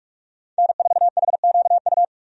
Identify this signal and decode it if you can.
Morse code, heard as a keyed tone, about 41 wpm, 700 Hz. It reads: N4HQU
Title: USA Call Signs - 5 Characters - 45wpm